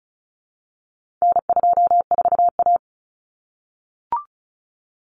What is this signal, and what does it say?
Morse code, heard as a keyed tone, about 34 wpm, 700 Hz. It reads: N24A
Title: Common Cut Numbers - 4-Digit 35wpm